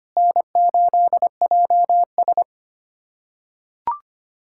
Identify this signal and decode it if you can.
Morse code, heard as a keyed tone, about 25 wpm, 700 Hz. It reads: N8JS
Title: USA Callsign DE Callsign - 4 Characters - 25wpm